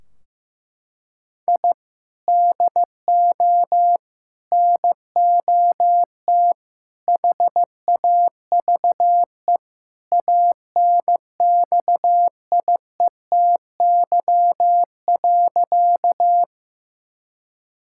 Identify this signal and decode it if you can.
Morse code, heard as a keyed tone, about 15 wpm, 700 Hz. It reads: I DO NOT HAVE ANXIETY.
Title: Sentences from Top 1600 Words 15wpm